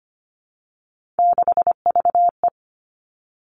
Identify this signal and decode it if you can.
Morse code, heard as a keyed tone, about 25 wpm, 700 Hz. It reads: 6VE
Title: Three Challenging Letter-Number Combinations 25wpm